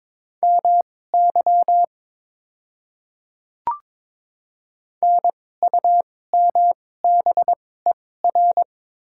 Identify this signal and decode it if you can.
Morse code, heard as a keyed tone, about 22 wpm, spaced as for 16 wpm, 700 Hz. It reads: MY NUMBER
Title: 2x Character Spacing - Top 100 Words - 22wpm